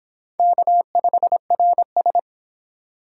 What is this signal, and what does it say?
Morse code, heard as a keyed tone, about 26 wpm, 700 Hz. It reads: K5RS